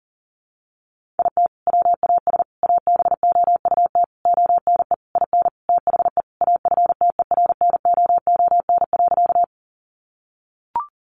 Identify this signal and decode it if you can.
Morse code, heard as a keyed tone, about 40 wpm, 700 Hz. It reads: IT WAS ABOUT ONE IN THE AFTERNOON.
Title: Sentences from Top 1500 Words 40wpm